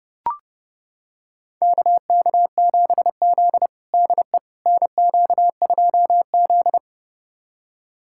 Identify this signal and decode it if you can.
Morse code, heard as a keyed tone, about 30 wpm, 700 Hz. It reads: KK7Z DE NQ2Z
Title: USA Callsign DE Callsign - 4 Characters - 30wpm